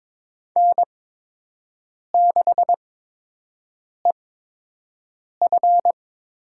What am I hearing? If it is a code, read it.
Morse code, heard as a keyed tone, about 22 wpm, spaced as for 6 wpm, 700 Hz. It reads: N6EF
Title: USA Call Signs - 8x Character Spacing - 22wpm